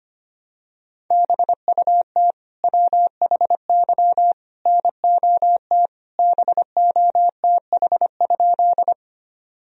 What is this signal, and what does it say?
Morse code, heard as a keyed tone, about 25 wpm, 700 Hz. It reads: BUT WHY NOT BOTH?
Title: Sentences from Top 1000 Words - Review Part 1 - 25wpm